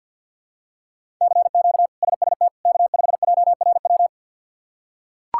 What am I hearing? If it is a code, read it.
Morse code, heard as a keyed tone, about 45 wpm, 700 Hz. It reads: CQ SST K5JRW
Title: K1USN SST QSO Practice - 50wpm